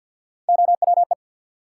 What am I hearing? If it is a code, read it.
Morse code, heard as a keyed tone, about 48 wpm, 700 Hz. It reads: OWE
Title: Three Letter Words - 50wpm